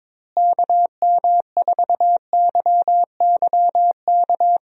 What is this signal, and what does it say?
Morse code, heard as a keyed tone, about 22 wpm, 700 Hz. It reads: KM4YYK
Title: USA Call Signs - 6 Characters - Repeated 2x - 22wpm